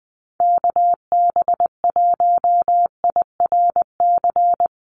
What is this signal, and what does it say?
Morse code, heard as a keyed tone, about 20 wpm, 700 Hz. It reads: KB1IRC
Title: USA Call Signs 20wpm